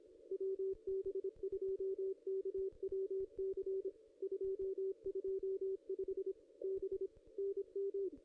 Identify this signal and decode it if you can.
Morse code, heard as a keyed tone, about 25 wpm, 440 Hz, with radio noise. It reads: WB2KWC 225 B NM
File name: contests / ss-cw-k5tr-op-n5rz -0400